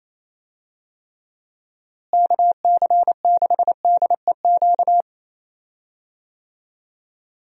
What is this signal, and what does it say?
Morse code, heard as a keyed tone, about 27 wpm, 700 Hz. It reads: KC6DEQ
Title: USA Call Signs - 6 Characters - Repeated 2x - 28wpm